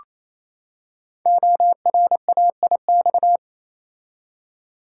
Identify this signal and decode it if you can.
Morse code, heard as a keyed tone, about 28 wpm, 700 Hz. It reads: ORAIX
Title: Code Groups - 5 Letters - 28wpm